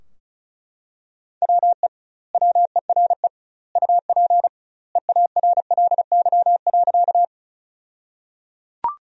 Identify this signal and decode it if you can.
Morse code, heard as a keyed tone, about 36 wpm, 700 Hz. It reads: WE WERE UP EARLY.
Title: 2x Word-Spacing - Sentences from Top 400 Words - 35wpm